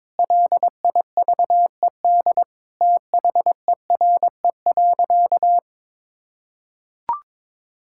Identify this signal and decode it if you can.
Morse code, heard as a keyed tone, about 22 wpm, 700 Hz. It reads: LIVED THERE.
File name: Sentences with 7-Word Phrases - 22wpm